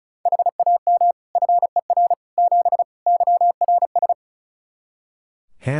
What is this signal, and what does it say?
Morse code, heard as a keyed tone, about 36 wpm, 700 Hz. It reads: HAM FER 7 YRS
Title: Practice QSO - Exchange 3 - More Info - 35wpm